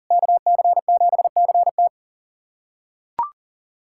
Morse code, heard as a keyed tone, about 40 wpm, 700 Hz. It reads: KC7CT
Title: USA Callsign DE Callsign - 5 Characters - 40wpm